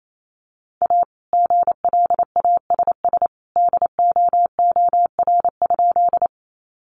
Morse code, heard as a keyed tone, about 28 wpm, 700 Hz. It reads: A GLASS DOOR?
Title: Sentences from Top 1300 Words 28wpm